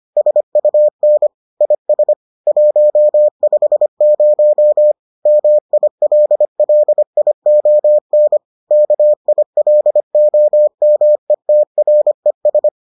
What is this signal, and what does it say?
Morse code, heard as a keyed tone, about 25 wpm, 600 Hz. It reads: SUN IS 150 MILLION KILOMETRES